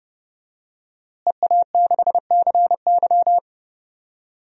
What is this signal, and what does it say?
Morse code, heard as a keyed tone, about 30 wpm, 700 Hz. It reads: EA6CY
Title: DXCC-100 Call Signs - 5 Characters - Repeated 2x - 30wpm